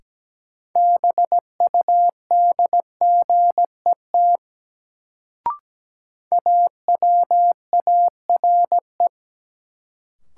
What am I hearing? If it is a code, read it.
Morse code, heard as a keyed tone, about 17 wpm, 700 Hz. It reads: BUDGET AWARE
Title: Sentences from Top 1200 Words - New Word Drill - 17wpm